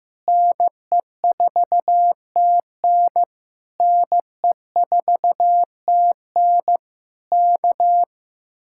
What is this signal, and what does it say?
Morse code, heard as a keyed tone, about 15 wpm, 700 Hz. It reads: NE4TN NE4TN K
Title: Speed-Racing Common QSO Phrases Encore - 15wpm